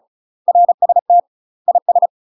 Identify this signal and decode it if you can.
Morse code, heard as a keyed tone, about 36 wpm, 700 Hz. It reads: RST IS
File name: UK MASTER QSOs TRAINING / 35WPM / UK MASTER QSOs 0041